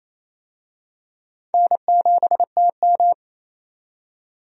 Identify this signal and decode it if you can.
Morse code, heard as a keyed tone, about 28 wpm, 700 Hz. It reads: N7TM
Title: USA Call Signs - 4 Characters - 28wpm